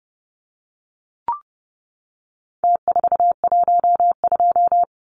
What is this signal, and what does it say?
Morse code, heard as a keyed tone, about 30 wpm, 700 Hz. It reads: T412
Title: Numbers - Contest Style - 4-Digit 30wpm